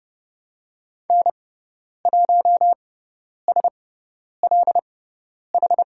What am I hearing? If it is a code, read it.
Morse code, heard as a keyed tone, about 30 wpm, spaced as for 10 wpm, 700 Hz. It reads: N1SLH
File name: Lesson 16g - Letter c - Call Signs 30wpm